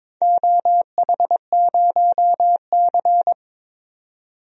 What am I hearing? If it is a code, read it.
Morse code, heard as a keyed tone, about 22 wpm, 700 Hz. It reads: OH0C
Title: DXCC-100 Call Signs - 4 Characters - 22wpm